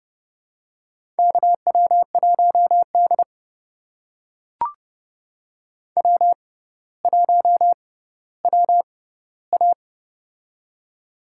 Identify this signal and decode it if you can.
Morse code, heard as a keyed tone, about 28 wpm, spaced as for 15 wpm, 700 Hz. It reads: KW1D W1WA
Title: USA Call Signs - 4 Characters - 6x Character Spacing - 30wpm